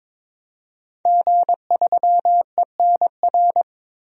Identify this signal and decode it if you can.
Morse code, heard as a keyed tone, about 22 wpm, 700 Hz. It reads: G3ENR
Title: DXCC-100 Call Signs - 5 Characters - 22wpm